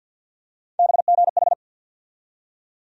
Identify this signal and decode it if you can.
Morse code, heard as a keyed tone, about 51 wpm, 700 Hz. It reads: BGH
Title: USA Call Sign Postfixes - 3 Letters 50wpm